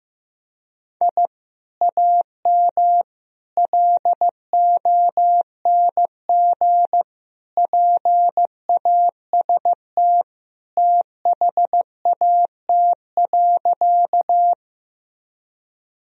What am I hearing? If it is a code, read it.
Morse code, heard as a keyed tone, about 16 wpm, 700 Hz. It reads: I AM LONG PAST THAT.
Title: Sentences from Top 1000 Words - Review Part 3 - 15wpm